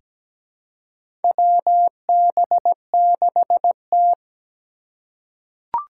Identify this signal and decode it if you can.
Morse code, heard as a keyed tone, about 17 wpm, 700 Hz. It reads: WB6T
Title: USA Call Signs - 3x Character Spacing - 17wpm